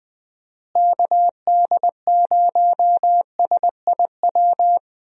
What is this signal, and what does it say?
Morse code, heard as a keyed tone, about 21 wpm, 700 Hz. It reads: KD0SIW
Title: USA Call Signs - 6 Characters - Repeated 2x - 20wpm